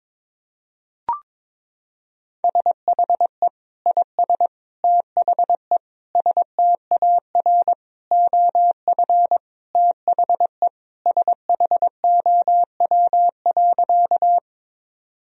Morse code, heard as a keyed tone, about 22 wpm, 700 Hz. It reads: SHE IS THE STAR OF THE SHOW.
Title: Sentences from Top 1000 Words - Review Part 2 - 22wpm